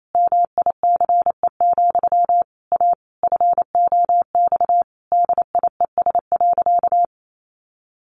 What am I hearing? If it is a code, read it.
Morse code, heard as a keyed tone, about 28 wpm, 700 Hz. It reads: MICE, A FOX DIES.
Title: A Sound Of Thunder 28wpm